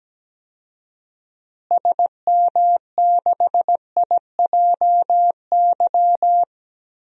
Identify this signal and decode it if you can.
Morse code, heard as a keyed tone, about 17 wpm, 700 Hz. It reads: SM6IJY
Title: DXCC-100 Call Signs - 6 Characters - Repeated 2x - 17wpm